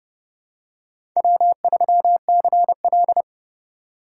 Morse code, heard as a keyed tone, about 29 wpm, 700 Hz. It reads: W3CL
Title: USA Call Signs - 4 Characters - 3x Character Spacing - 30wpm